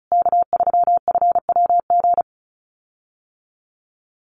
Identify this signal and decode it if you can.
Morse code, heard as a keyed tone, about 34 wpm, 700 Hz. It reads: K3FWG
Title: USA Call Signs Repeated 3x 35wpm